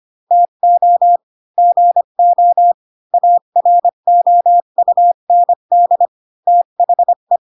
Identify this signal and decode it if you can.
Morse code, heard as a keyed tone, about 25 wpm, 700 Hz. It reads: TO GO AROUND THE